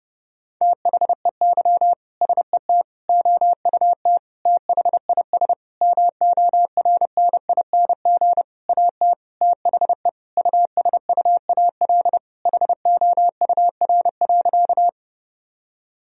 Morse code, heard as a keyed tone, about 30 wpm, 700 Hz. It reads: THEY SET OUT THIS MORNING AT THE USUAL HOUR.
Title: Sentences from Top 1000 Words - Review Part 2 - 30wpm